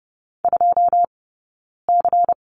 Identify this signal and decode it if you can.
Morse code, heard as a keyed tone, about 30 wpm, 700 Hz. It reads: 2 C
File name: Lesson 18d - Letter m - Two Characters 30wpm